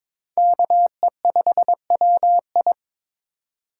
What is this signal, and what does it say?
Morse code, heard as a keyed tone, about 22 wpm, 700 Hz. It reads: KE5WI
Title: USA Call Signs - 5 Characters - 4x Character Spacing - 22wpm